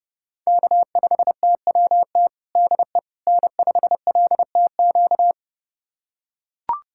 Morse code, heard as a keyed tone, about 30 wpm, 700 Hz. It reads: K5TWT DE N5LTQ
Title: USA Callsign DE Callsign - 5 Characters - 30wpm